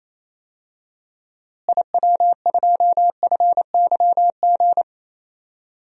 Morse code, heard as a keyed tone, about 28 wpm, 700 Hz. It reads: IW2FYG
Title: DXCC-100 Call Signs - 6 Characters - Repeated 3x - 28wpm